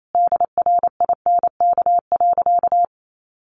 Morse code, heard as a keyed tone, about 28 wpm, 700 Hz. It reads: DRINK.
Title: Sentences from Top 1500 Words 28wpm